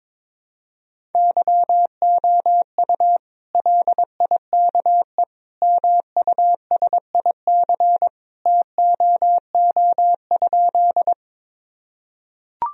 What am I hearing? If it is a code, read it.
Morse code, heard as a keyed tone, about 21 wpm, 700 Hz. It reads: YOU LIKE MUSIC TOO?
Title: Sentences from Top 1000 Words - Review Part 1 - 22wpm